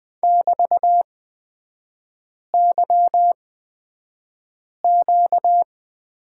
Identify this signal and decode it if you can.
Morse code, heard as a keyed tone, about 20 wpm, spaced as for 5 wpm, 700 Hz. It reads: <BT>YQ
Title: 236 Lesson 38e - Letter q - Three Characters 20wpm